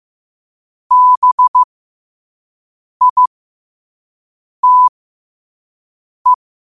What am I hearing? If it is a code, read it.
Morse code, heard as a keyed tone, about 15 wpm, spaced as for 5 wpm, 1000 Hz. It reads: BITE
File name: qotdfiles / 5wpm